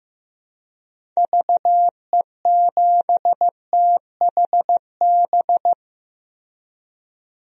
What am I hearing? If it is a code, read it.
Morse code, heard as a keyed tone, about 15 wpm, 700 Hz. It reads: VE7THB
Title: DXCC-100 Call Signs - 6 Characters - Repeated 2x - 15wpm